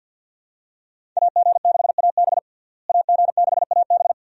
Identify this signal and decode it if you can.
Morse code, heard as a keyed tone, about 51 wpm, 700 Hz. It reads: AG6AB AG6AB
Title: USA Call Signs - 5 Characters - Repeated 2x - 50wpm